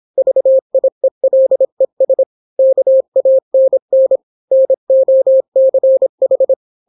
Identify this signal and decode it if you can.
Morse code, heard as a keyed tone, about 26 wpm, 530 Hz. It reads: VIELES KANN NOCH